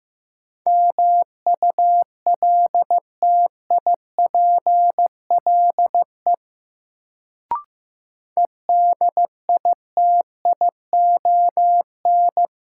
Morse code, heard as a keyed tone, about 15 wpm, 700 Hz. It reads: MULTIPLE EDITION
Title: Sets of 5 Words - New Word Drill - 15wpm